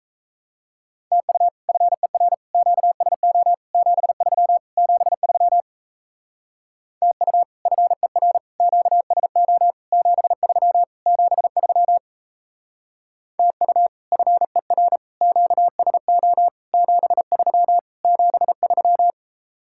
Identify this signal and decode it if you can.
Morse code, heard as a keyed tone, about 34 wpm, 700 Hz. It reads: TU FER QSO 73 73 TU FER QSO 73 73 TU FER QSO 73 73
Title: Speed-Racing Common QSO Phrases Encore - 28wpm